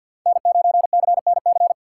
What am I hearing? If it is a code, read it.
Morse code, heard as a keyed tone, about 52 wpm, 700 Hz. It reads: N9CNC
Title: USA Call Signs - 2x Character Spacing - 50wpm